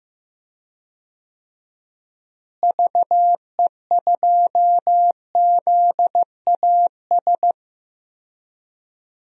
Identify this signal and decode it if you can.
Morse code, heard as a keyed tone, about 15 wpm, 700 Hz. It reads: VE2ZAS
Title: DXCC-100 Call Signs - 6 Characters - Repeated 2x - 15wpm